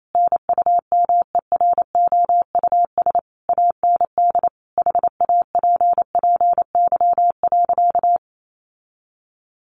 Morse code, heard as a keyed tone, about 28 wpm, 700 Hz. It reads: NUMEROUS AND HAPPY.
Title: Sentences from Top 1400 Words 28wpm